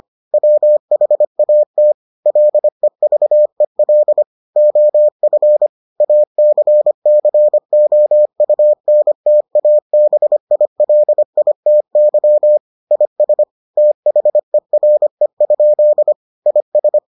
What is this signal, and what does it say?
Morse code, heard as a keyed tone, about 25 wpm, 600 Hz. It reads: WHAT LEVEL OF ACCOUNTABILITY IS THERE? IS